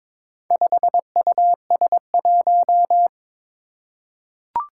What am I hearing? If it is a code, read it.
Morse code, heard as a keyed tone, about 22 wpm, 700 Hz. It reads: 5US1
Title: Four Challenging Letter-Number Combinations 22wpm